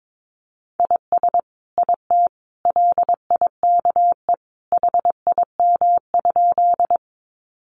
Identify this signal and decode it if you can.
Morse code, heard as a keyed tone, about 22 wpm, 700 Hz. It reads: IS IT LIKE HIM?
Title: Sentences from Top 100 Words 22wpm